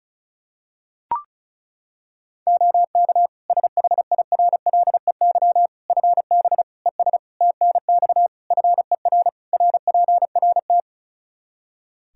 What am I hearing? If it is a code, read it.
Morse code, heard as a keyed tone, about 36 wpm, 700 Hz. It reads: OK SHIRLEY FB ES TNX FER RPRT
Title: Practice QSO - Exchange 2 - Info - 35wpm